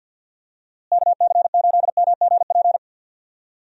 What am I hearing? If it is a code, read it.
Morse code, heard as a keyed tone, about 52 wpm, 700 Hz. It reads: KC8MGP
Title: USA Call Signs - 6 Characters - Repeated 2x - 50wpm